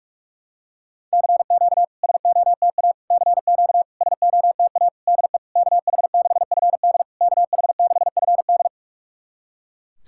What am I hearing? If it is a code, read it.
Morse code, heard as a keyed tone, about 46 wpm, 690 Hz. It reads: CQ SOTA CQ SOTA DE KH6FD KH6FD
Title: SOTA QSO Practice 45wpm